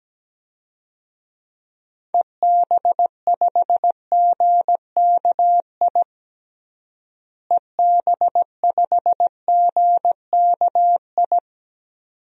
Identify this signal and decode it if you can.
Morse code, heard as a keyed tone, about 17 wpm, 700 Hz. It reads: EB5GKI EB5GKI
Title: DXCC-100 Call Signs - 6 Characters - Repeated 2x - 17wpm